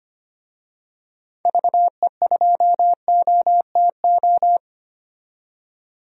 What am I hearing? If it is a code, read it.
Morse code, heard as a keyed tone, about 25 wpm, 700 Hz. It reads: VE2OTO
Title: DXCC-100 Call Signs - 6 Characters - 25wpm